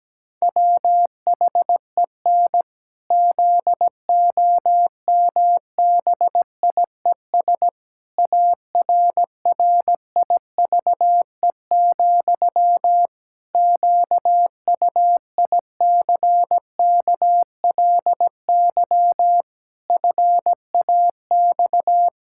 Morse code, heard as a keyed tone, about 17 wpm, 700 Hz. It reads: WHEN ZOMBIES ARRIVE, QUICKLY FAX
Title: Pangram Challenge v2 - 17wpm